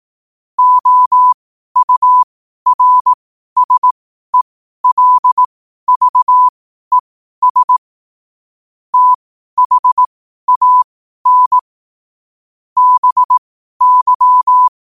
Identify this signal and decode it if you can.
Morse code, heard as a keyed tone, about 18 wpm, spaced as for 12 wpm, 1000 Hz. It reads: OURSELVES THAN BY
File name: qotdfiles / 12wpm